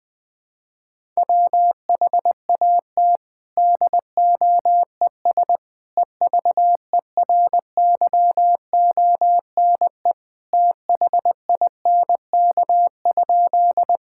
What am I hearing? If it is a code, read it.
Morse code, heard as a keyed tone, about 20 wpm, 700 Hz. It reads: WHAT DOES EVERYONE THINK?
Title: Sentences from Top 1000 Words - Review Part 5 - 20wpm